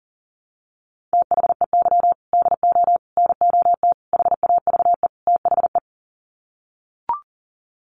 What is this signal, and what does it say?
Morse code, heard as a keyed tone, about 40 wpm, 700 Hz. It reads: THEY DO NOT HAVE THE
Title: Sets of 5 Words 40wpm